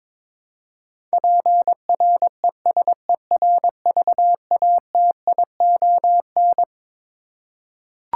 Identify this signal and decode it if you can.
Morse code, heard as a keyed tone, about 22 wpm, 700 Hz. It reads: PRESERVATION
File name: Twelve Letter Words - 22wpm